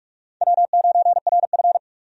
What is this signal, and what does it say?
Morse code, heard as a keyed tone, about 48 wpm, 700 Hz. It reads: W9RF
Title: USA Call Signs - 4 Characters - 45wpm